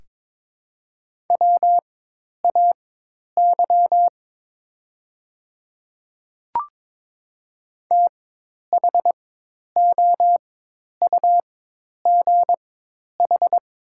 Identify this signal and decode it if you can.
Morse code, heard as a keyed tone, about 22 wpm, spaced as for 10 wpm, 700 Hz. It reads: WAY THOUGH
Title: 4x Character Spacing - Top 500 Words - 22wpm